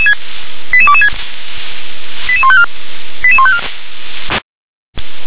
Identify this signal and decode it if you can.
Five-tone selective call (ZVEI), heard as quick runs of five tones, tones of about 70 ms. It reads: 88226 88154 88154
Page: /357/SDR/ZVEI/